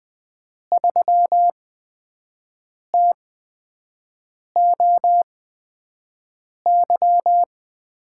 Morse code, heard as a keyed tone, about 20 wpm, spaced as for 5 wpm, 700 Hz. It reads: 3TOY
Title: USA Call Signs - 5 Characters - 8x Character Spacing - 20wpm